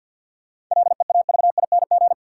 Morse code, heard as a keyed tone, about 53 wpm, 700 Hz. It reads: LEAVING
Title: Sentences from Top 1000 Words - Review Part 4 - New Word Drill - 50wpm